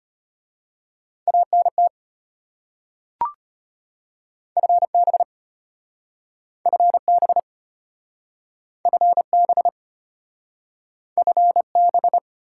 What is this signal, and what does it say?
Morse code, heard as a keyed tone, about 34 wpm, 700 Hz. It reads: ANT FB FB FB FB